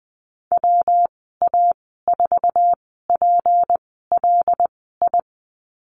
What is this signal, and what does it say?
Morse code, heard as a keyed tone, about 20 wpm, spaced as for 15 wpm, 700 Hz. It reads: WA4PLI
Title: 193 Lesson 31g - Letter v - Call Signs 20wpm